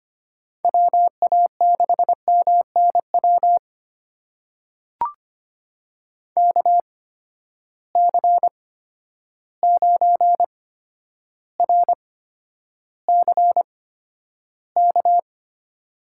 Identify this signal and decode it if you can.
Morse code, heard as a keyed tone, about 25 wpm, spaced as for 11 wpm, 700 Hz. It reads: WA6MNW KC9RCK
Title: USA Call Signs - 6 Characters - 8x Character Spacing - 25wpm